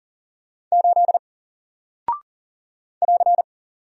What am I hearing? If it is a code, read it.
Morse code, heard as a keyed tone, about 40 wpm, 700 Hz. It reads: 8 <AR>
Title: Single Character (Letter:Number:Punctuation:Prosign) Alt 40wpm